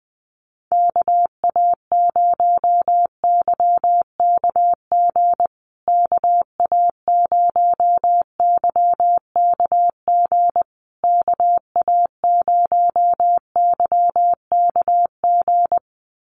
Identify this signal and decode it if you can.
Morse code, heard as a keyed tone, about 20 wpm, 700 Hz. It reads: KA0YKG KA0YKG KA0YKG
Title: USA Call Signs Repeated 3x 20wpm